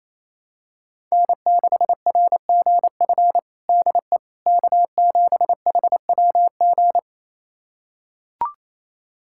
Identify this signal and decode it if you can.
Morse code, heard as a keyed tone, about 28 wpm, 700 Hz. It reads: N6RGF DE K7HWG
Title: USA Callsign DE Callsign - 5 Characters - 28wpm